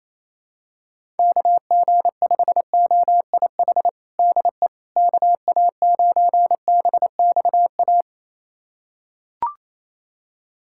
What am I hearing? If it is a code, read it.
Morse code, heard as a keyed tone, about 28 wpm, 700 Hz. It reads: KG5OIH DE KA9BXA
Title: USA Callsign DE Callsign - 6 Characters - 28wpm